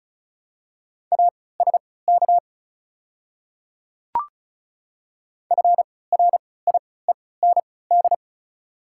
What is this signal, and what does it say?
Morse code, heard as a keyed tone, about 36 wpm, spaced as for 20 wpm, 700 Hz. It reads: ASK FRIEND
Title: 3x Character Spacing - Top 500 Words - 35wpm